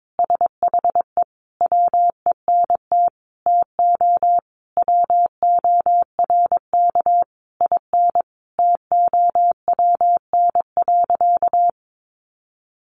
Morse code, heard as a keyed tone, about 22 wpm, 700 Hz. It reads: SHE WENT TO WORK IN TOWN.
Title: Sentences from Top 1500 Words 22wpm